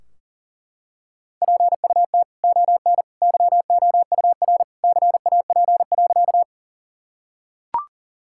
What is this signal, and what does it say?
Morse code, heard as a keyed tone, about 40 wpm, 700 Hz. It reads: PUT ON YOUR CAP.
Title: Sentences from Top 1600 Words 40wpm